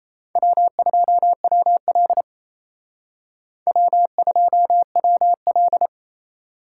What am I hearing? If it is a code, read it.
Morse code, heard as a keyed tone, about 36 wpm, 700 Hz. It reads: W2WL W2WL